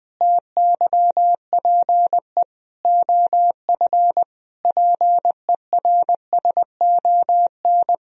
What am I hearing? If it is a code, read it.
Morse code, heard as a keyed tone, about 21 wpm, 700 Hz. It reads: TYPE OF PERSON
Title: Sentences from Top 1000 Words - Review Part 3 - 20wpm